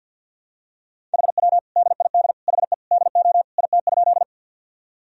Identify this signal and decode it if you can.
Morse code, heard as a keyed tone, about 51 wpm, 700 Hz. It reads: HW DID HE DO IT?
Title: Sentences from Top 100 Words and Abbreviations 50wpm